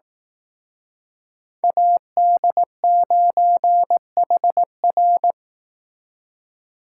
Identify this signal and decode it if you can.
Morse code, heard as a keyed tone, about 18 wpm, 700 Hz. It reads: AD9HR
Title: Speed Racing - USA 2x2 Call Signs 15wpm